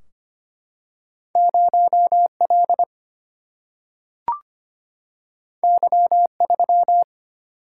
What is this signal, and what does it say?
Morse code, heard as a keyed tone, about 25 wpm, 700 Hz. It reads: 0 L Y 3
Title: Single-Letter Single-Number 25wpm